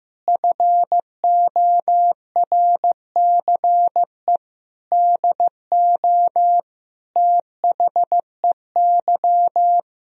Morse code, heard as a keyed tone, about 15 wpm, 700 Hz. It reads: FORCE DO THEY
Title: Sentences from Top 1000 Words - Review Part 3 - 15wpm